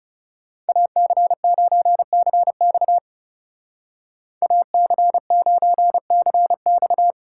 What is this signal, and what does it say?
Morse code, heard as a keyed tone, about 36 wpm, 700 Hz. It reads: AC9CX AC9CX
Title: Speed Racing - USA 2x2 Call Signs 30wpm